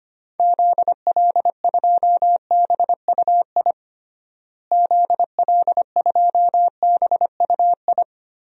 Morse code, heard as a keyed tone, about 25 wpm, 700 Hz. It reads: ZL2BUI ZL2BUI
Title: DXCC-100 Call Signs - 6 Characters - Repeated 2x - 25wpm